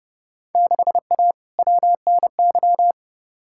Morse code, heard as a keyed tone, about 30 wpm, 700 Hz. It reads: 6A WNY
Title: Field Day Exchanges - 30wpm